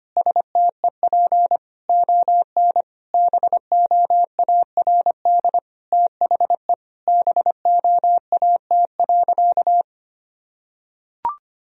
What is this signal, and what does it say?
Morse code, heard as a keyed tone, about 25 wpm, 700 Hz. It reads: STEP ON BOARD THE BOAT.
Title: Sentences from Top 1000 Words - Review Part 4 - 25wpm